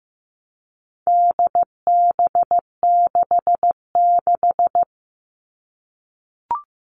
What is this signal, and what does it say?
Morse code, heard as a keyed tone, about 15 wpm, 700 Hz. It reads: DB66
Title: All Cut Numbers - 4-Digit 15wpm